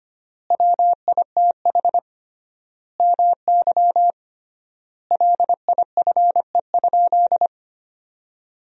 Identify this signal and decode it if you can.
Morse code, heard as a keyed tone, about 25 wpm, 700 Hz. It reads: WITH MY LIFE?
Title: 3x Word-Spacing - Sentences from Top 300 Words - 25wpm